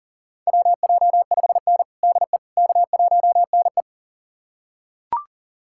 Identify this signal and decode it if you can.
Morse code, heard as a keyed tone, about 40 wpm, 700 Hz. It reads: WJ5N DE K1NE
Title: USA Callsign DE Callsign - 4 Characters - 40wpm